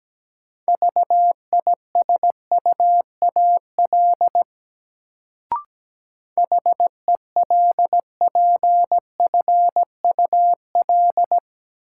Morse code, heard as a keyed tone, about 17 wpm, 700 Hz. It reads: VISUAL HELPFUL
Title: Sentences from Top 1000 Words - Review Part 6 - New Word Drill - 17wpm